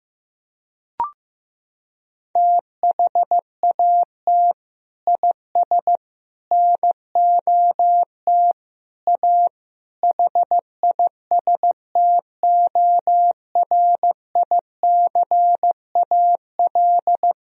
Morse code, heard as a keyed tone, about 16 wpm, 700 Hz. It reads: THAT IS NOT A HISTORICAL
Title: Sentences from Top 1000 Words - Review Part 4 - 15wpm